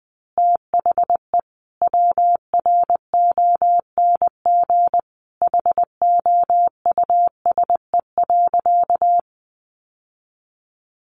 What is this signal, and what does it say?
Morse code, heard as a keyed tone, about 20 wpm, 700 Hz. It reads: THE WRONG HOUSE.
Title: Sentences from Top 900 Words 20wpm